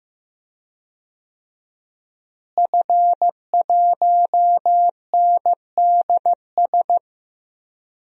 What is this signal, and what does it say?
Morse code, heard as a keyed tone, about 15 wpm, 700 Hz. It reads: F1NDS
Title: DXCC-100 Call Signs - 5 Characters - Repeated 3x - 15wpm